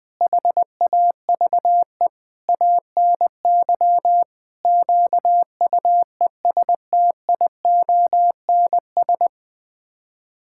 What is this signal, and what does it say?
Morse code, heard as a keyed tone, about 20 wpm, 700 Hz. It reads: HAVE ANY QUESTIONS
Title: Sets of 8 Words - 20wpm